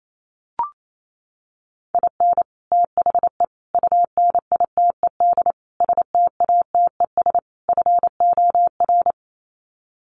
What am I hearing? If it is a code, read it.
Morse code, heard as a keyed tone, about 28 wpm, 700 Hz. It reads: IN THE UNITED STATES FOR
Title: Sets of 5 Words Encore2 28wpm